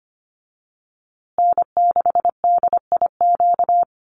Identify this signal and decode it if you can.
Morse code, heard as a keyed tone, about 25 wpm, 700 Hz. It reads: N6DIQ
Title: USA Call Signs Repeated 3x 25wpm